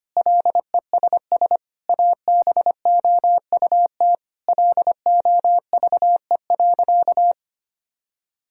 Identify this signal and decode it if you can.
Morse code, heard as a keyed tone, about 25 wpm, 700 Hz. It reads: LESS ABOUT LOVE.
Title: Sentences from Top 1000 Words - Review Part 2 - 25wpm